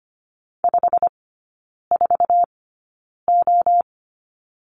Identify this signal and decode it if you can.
Morse code, heard as a keyed tone, about 25 wpm, spaced as for 9 wpm, 690 Hz. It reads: 54O
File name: Lesson 15e - Number 5 - Three Characters 25wpm